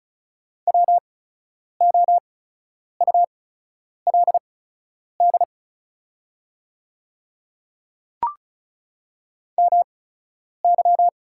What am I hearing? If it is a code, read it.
Morse code, heard as a keyed tone, about 34 wpm, spaced as for 10 wpm, 700 Hz. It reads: WOULD MY
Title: 8x Character Spacing - Top 200 Words - 35wpm